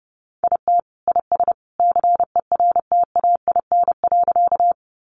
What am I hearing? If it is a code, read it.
Morse code, heard as a keyed tone, about 30 wpm, 700 Hz. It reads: IT IS CERTAIN.
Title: Sentences from Top 500 Words 30wpm